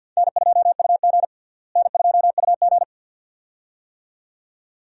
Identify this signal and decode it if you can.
Morse code, heard as a keyed tone, about 50 wpm, 700 Hz. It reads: N2UG N2UG
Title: USA Call Signs - 4 Characters - Repeated 2x - 50wpm